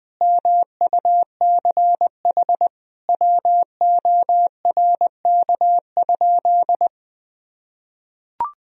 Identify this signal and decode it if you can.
Morse code, heard as a keyed tone, about 20 wpm, 700 Hz. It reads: MUCH WORK?
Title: Sentences from Top 1000 Words - Review Part 1 - 20wpm